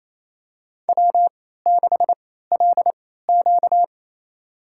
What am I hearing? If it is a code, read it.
Morse code, heard as a keyed tone, about 28 wpm, spaced as for 16 wpm, 700 Hz. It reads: W6LQ
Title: USA Call Signs - 4 Characters - 3x Character Spacing - 28wpm